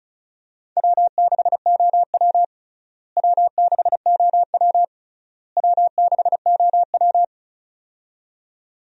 Morse code, heard as a keyed tone, about 36 wpm, 700 Hz. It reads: W6OW W6OW W6OW
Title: USA Call Signs - 4 Characters - Repeated 3x - 35wpm